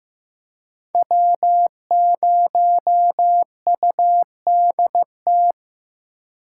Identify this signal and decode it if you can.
Morse code, heard as a keyed tone, about 18 wpm, 700 Hz. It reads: W0UDT
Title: Speed Racing - USA 1x3 Call Signs 15wpm